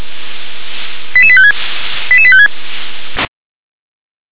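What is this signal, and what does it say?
Five-tone selective call (ZVEI), heard as quick runs of five tones, tones of about 70 ms. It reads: 88856 88856
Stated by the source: ZVEI five-tone selective call received on an SDR.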